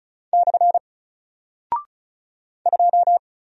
Morse code, heard as a keyed tone, about 36 wpm, 700 Hz. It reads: / 2
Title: Single Character (Letter:Number:Punctuation:Prosign) Alt 35wpm